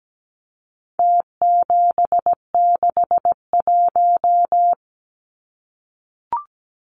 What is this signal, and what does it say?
Morse code, heard as a keyed tone, about 17 wpm, 700 Hz. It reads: T761
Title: Numbers - Contest Style - 4-Digit 17wpm